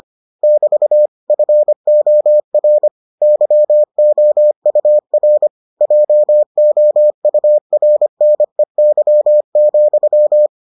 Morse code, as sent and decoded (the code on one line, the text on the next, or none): -...- / ..-. --- .-. / -.-- --- ..- .-. / .--- --- ..- .-. -. -.-- --..--
<BT> FOR YOUR JOURNEY,